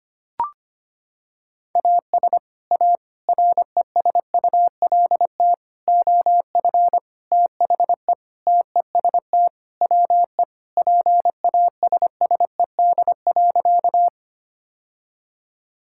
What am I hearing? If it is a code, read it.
Morse code, heard as a keyed tone, about 25 wpm, 700 Hz. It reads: AS A RESULT OF THE TEST WE PASSED.
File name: practice / 25wpm / Sentences with 4-Word Phrases - 25wpm